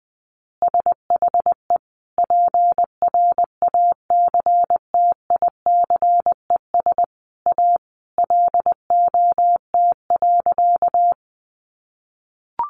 Morse code, heard as a keyed tone, about 20 wpm, 700 Hz. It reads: SHE PRACTICES A LOT.
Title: Additional Easy Phrases 20wpm